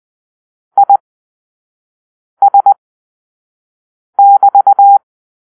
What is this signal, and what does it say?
Morse code, heard as a keyed tone, about 20 wpm, spaced as for 5 wpm, 800 Hz. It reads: IS<BT>